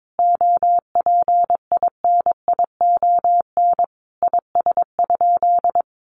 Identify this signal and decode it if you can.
Morse code, heard as a keyed tone, about 22 wpm, 700 Hz. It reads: OPINION IS?
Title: Sentences from Top 1200 Words 22wpm